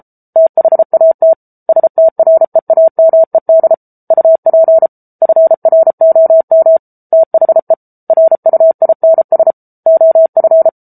Morse code, heard as a keyed tone, about 34 wpm, 650 Hz. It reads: THAT STREAMED UP FROM THE RUINS OF
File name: Book Two / 07-Chapter 7 - 33 WPM